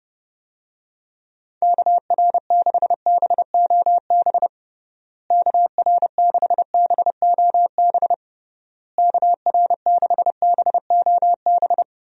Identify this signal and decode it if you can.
Morse code, heard as a keyed tone, about 30 wpm, 700 Hz. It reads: KR6BOB KR6BOB KR6BOB
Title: USA Call Signs - 6 Characters - Repeated 3x - 30wpm